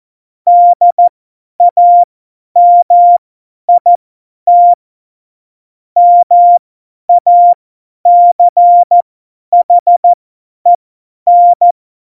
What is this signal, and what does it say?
Morse code, heard as a keyed tone, about 14 wpm, spaced as for 10 wpm, 700 Hz. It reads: DAMIT MACHEN
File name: Morsekurs / Witze / Witz 13. Oktober 2020, 50 F70